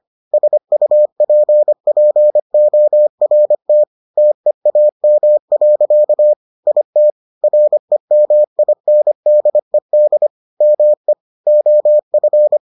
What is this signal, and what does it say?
Morse code, heard as a keyed tone, about 25 wpm, 600 Hz. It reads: SUPPORT TEAM. IT REMINDED ME OF